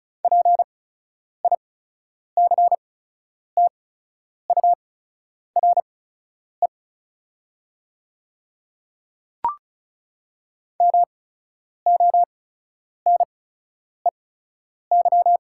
Morse code, heard as a keyed tone, about 36 wpm, spaced as for 10 wpm, 700 Hz. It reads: PICTURE MONEY
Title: 8x Character Spacing - Top 500 Words - 35wpm